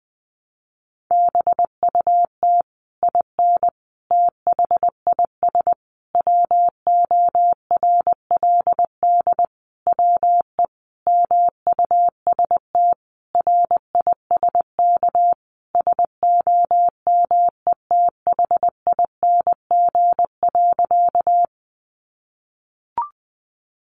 Morse code, heard as a keyed tone, about 19 wpm, 700 Hz. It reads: BUT IN THIS WORLD WE MUST RISK SOMETHING.
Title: Sentences from Top 600 Words 20wpm